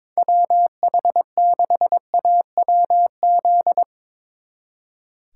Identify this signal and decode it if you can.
Morse code, heard as a keyed tone, about 24 wpm, 700 Hz. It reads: WH6AWZ
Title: Speed Racing - USA 2x3 Call Signs 22wpm